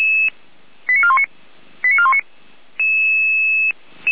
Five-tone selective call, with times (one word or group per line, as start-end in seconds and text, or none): 0.9-1.2 89419
1.8-2.2 89419